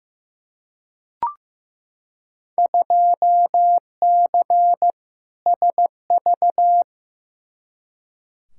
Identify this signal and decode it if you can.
Morse code, heard as a keyed tone, about 15 wpm, 700 Hz. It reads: 2C SV
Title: Field Day Exchanges Encore - 15wpm